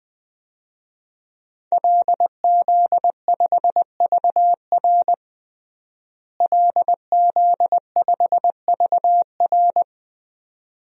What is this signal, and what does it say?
Morse code, heard as a keyed tone, about 20 wpm, 700 Hz. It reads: LZ5VR LZ5VR
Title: DXCC-100 Call Signs - 5 Characters - Repeated 2x - 20wpm